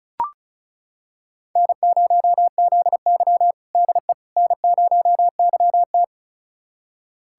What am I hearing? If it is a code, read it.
Morse code, heard as a keyed tone, about 36 wpm, 700 Hz. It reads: N0ZY DE N0YT
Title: USA Callsign DE Callsign - 4 Characters - 35wpm